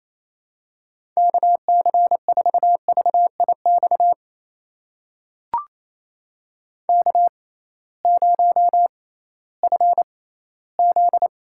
Morse code, heard as a keyed tone, about 27 wpm, spaced as for 16 wpm, 700 Hz. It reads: KC4VIX K0FZ
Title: USA Call Signs - 6x Character Spacing - 28wpm